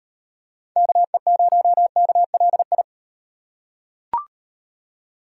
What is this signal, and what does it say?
Morse code, heard as a keyed tone, about 38 wpm, 700 Hz. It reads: KE0KLI